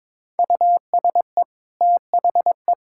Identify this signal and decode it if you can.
Morse code, heard as a keyed tone, about 21 wpm, 700 Hz. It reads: USE THE
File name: practice / 22wpm / Easy Sentences with Hints - 22wpm